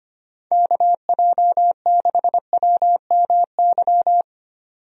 Morse code, heard as a keyed tone, about 25 wpm, 700 Hz. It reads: KJ6WMY
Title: USA Call Signs - 6 Characters - 25wpm